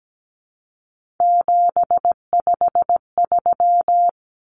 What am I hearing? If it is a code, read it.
Morse code, heard as a keyed tone, about 17 wpm, 700 Hz. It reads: 753
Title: Common Cut Numbers - 3-Digit 17wpm